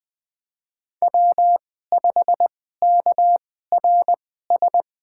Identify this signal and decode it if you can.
Morse code, heard as a keyed tone, about 21 wpm, spaced as for 15 wpm, 700 Hz. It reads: W5KRS
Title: USA Call Signs - 5 Characters - 2x Character Spacing - 20wpm